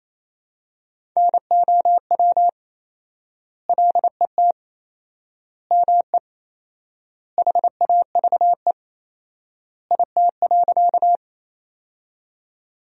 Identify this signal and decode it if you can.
Morse code, heard as a keyed tone, about 27 wpm, 700 Hz. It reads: NOW LET ME HAVE IT.
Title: 4x Word-Spacing - Sentences from Top 300 Words - 28wpm